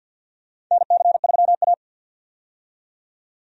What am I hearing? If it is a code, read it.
Morse code, heard as a keyed tone, about 53 wpm, 700 Hz. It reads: NC3A
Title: USA Call Signs - 4 Characters - 50wpm